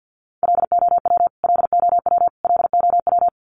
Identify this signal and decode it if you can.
Morse code, heard as a keyed tone, about 50 wpm, 700 Hz. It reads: LOW LOW LOW
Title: Top 200 Words Repeated 50wpm